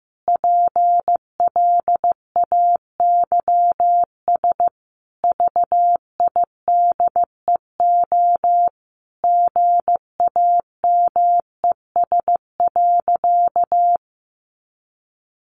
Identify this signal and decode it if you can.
Morse code, heard as a keyed tone, about 14 wpm, 700 Hz. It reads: PLAYS VIDEO GAMES.
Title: Additional Easy Phrases 15wpm